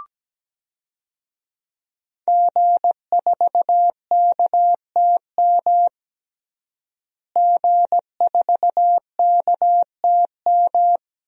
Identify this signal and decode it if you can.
Morse code, heard as a keyed tone, about 17 wpm, 700 Hz. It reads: G4KTM G4KTM
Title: DXCC-100 Call Signs - 5 Characters - Repeated 3x - 17wpm